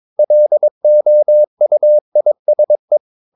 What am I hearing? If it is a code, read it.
Morse code, heard as a keyed tone, about 22 wpm, 600 Hz. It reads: LOUISE
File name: Ferie QRS 22wpm